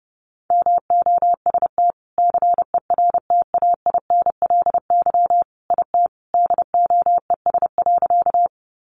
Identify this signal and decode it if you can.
Morse code, heard as a keyed tone, about 30 wpm, 700 Hz. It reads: MOST CERTAINLY IT DOES.
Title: Sentences from Top 900 Words 30wpm